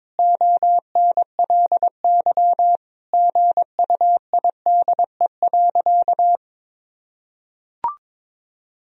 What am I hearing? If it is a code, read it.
Morse code, heard as a keyed tone, about 22 wpm, 700 Hz. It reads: ONLY GUIDE.
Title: Sentences from Top 1000 Words - Review Part 3 - 22wpm